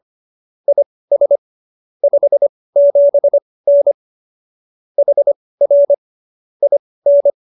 Morse code, heard as a keyed tone, about 25 wpm, spaced as for 18 wpm, 570 Hz. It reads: IS 57N HR IN